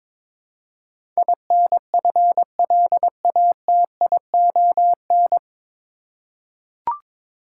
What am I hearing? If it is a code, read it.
Morse code, heard as a keyed tone, about 22 wpm, 700 Hz. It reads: INFLATION
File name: 9 Letter Words - First 3 Letters Given - 22wpm